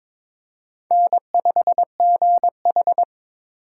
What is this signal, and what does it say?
Morse code, heard as a keyed tone, about 22 wpm, 700 Hz. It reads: N5GH
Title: USA Call Signs - 4 Characters - 6x Character Spacing - 22wpm